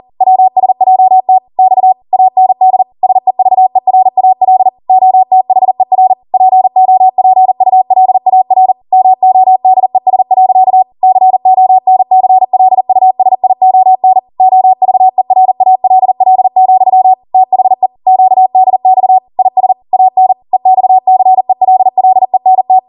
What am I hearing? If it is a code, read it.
Morse code, heard as a keyed tone, about 40 wpm, 750 Hz. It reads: WSJT X AND SEVERAL OTHER POPULAR MODES. CONCLUSION OVERALL, THE QDX IS AN EXCELLENT